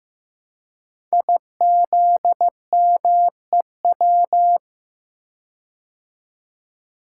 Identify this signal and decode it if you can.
Morse code, heard as a keyed tone, about 15 wpm, 700 Hz. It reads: IZMEW
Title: Code Groups - 5 Letters - 15wpm